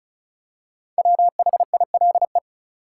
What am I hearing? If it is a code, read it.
Morse code, heard as a keyed tone, about 34 wpm, 700 Hz. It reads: WHILE
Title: Five Letter Words - 35wpm